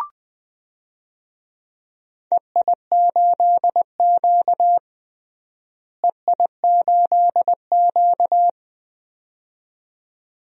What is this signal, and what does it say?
Morse code, heard as a keyed tone, about 21 wpm, 700 Hz. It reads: EI8Q EI8Q
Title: DXCC-100 Call Signs - 4 Characters - Repeated 2x - 20wpm